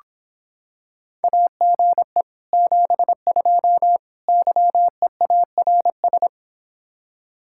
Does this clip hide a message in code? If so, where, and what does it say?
Yes: Morse code from 1.2 to 6.3 s, heard as a keyed tone, about 26 wpm, 700 Hz. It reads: AGE 72 YEARS